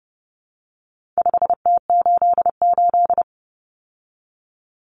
Morse code, heard as a keyed tone, about 30 wpm, 700 Hz. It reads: 5T88
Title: Common Cut Numbers - 4-Digit 30wpm